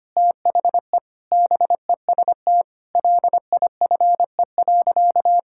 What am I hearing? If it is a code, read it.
Morse code, heard as a keyed tone, about 25 wpm, 700 Hz. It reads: THE BEST LIFE.
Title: Sentences from Top 1000 Words - Review Part 6 - 25wpm